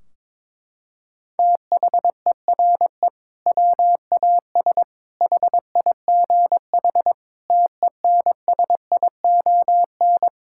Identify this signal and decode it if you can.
Morse code, heard as a keyed tone, about 22 wpm, 700 Hz. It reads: THERE WAS HIGH TENSION
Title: Sentences from Top 1600 Words 22wpm